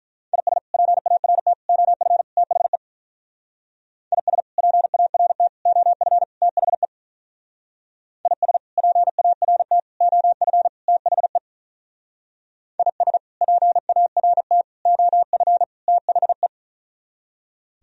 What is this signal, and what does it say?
Morse code, heard as a keyed tone, about 45 wpm, 700 Hz. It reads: IS PART OF THE IS PART OF THE IS PART OF THE IS PART OF THE